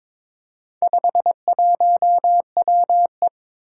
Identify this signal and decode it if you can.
Morse code, heard as a keyed tone, about 22 wpm, 700 Hz. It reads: 51WE
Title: Four Challenging Letter-Number Combinations 22wpm